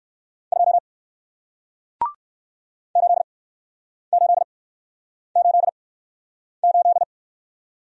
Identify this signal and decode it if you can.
Morse code, heard as a keyed tone, about 61 wpm, spaced as for 9 wpm, 700 Hz. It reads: 2 8888